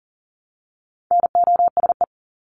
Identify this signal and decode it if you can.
Morse code, heard as a keyed tone, about 39 wpm, 700 Hz. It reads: NOSE
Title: Top 1501-1750 Words 40wpm